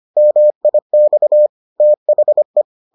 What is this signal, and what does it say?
Morse code, heard as a keyed tone, about 25 wpm, 600 Hz. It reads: MIX THE